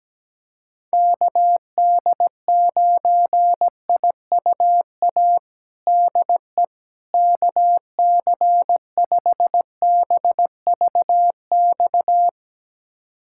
Morse code, heard as a keyed tone, about 17 wpm, 700 Hz. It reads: KD9IUA DE KC5BVX
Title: USA Callsign DE Callsign 17wpm